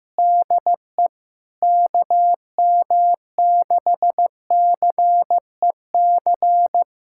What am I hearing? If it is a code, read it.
Morse code, heard as a keyed tone, about 16 wpm, 700 Hz. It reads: DE KM6CEC
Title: USA Callsign DE Callsign 15wpm